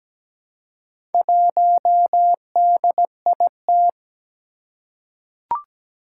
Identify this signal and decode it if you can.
Morse code, heard as a keyed tone, about 17 wpm, 700 Hz. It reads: 1DIT
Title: Four Challenging Letter-Number Combinations 17wpm